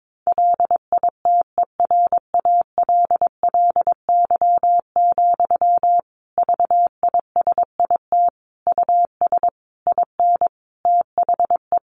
Morse code, heard as a keyed tone, about 22 wpm, 700 Hz. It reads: LITERALLY, VISIT US IN THE
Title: Sentences from Top 1400 Words 22wpm